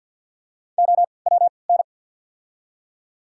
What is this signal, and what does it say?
Morse code, heard as a keyed tone, about 53 wpm, spaced as for 29 wpm, 690 Hz. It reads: OWN
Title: 3x Character Spacing - Top 300 Words - 50wpm